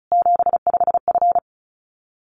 Morse code, heard as a keyed tone, about 34 wpm, 700 Hz. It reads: 75F
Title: Common QSO Phrases 35wpm